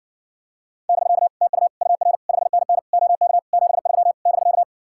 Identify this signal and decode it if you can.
Morse code, heard as a keyed tone, about 60 wpm, 700 Hz. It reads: <BK> TU UR 5NN OK 73 <BK>